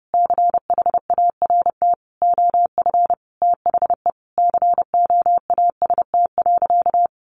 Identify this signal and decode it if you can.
Morse code, heard as a keyed tone, about 30 wpm, 700 Hz. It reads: CHART OF THE COAST.
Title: Sentences from Top 1300 Words 30wpm